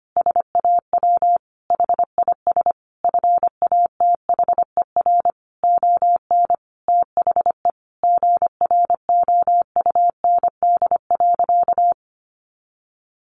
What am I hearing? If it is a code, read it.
Morse code, heard as a keyed tone, about 25 wpm, 700 Hz. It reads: SAW HIS FATHER ON THE GROUND.
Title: Sentences from Top 500 Words 25wpm